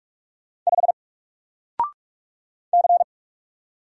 Morse code, heard as a keyed tone, about 44 wpm, 700 Hz. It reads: 5 C
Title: Single Character (Letter:Number:Punctuation:Prosign) 45wpm